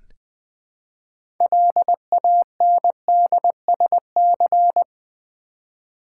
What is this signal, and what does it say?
Morse code, heard as a keyed tone, about 20 wpm, 700 Hz. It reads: LANDSC
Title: Build A Word - 3 Letters At A Time - 9-Letter Words - 20wpm